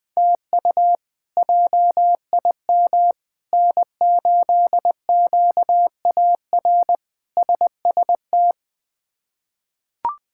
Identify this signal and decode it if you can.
Morse code, heard as a keyed tone, about 20 wpm, 700 Hz. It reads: TU JIM N8QAR SST
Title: K1USN SST QSO Practice - 20wpm